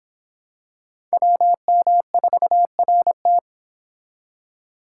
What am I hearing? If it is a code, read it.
Morse code, heard as a keyed tone, about 26 wpm, 700 Hz. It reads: WM4RT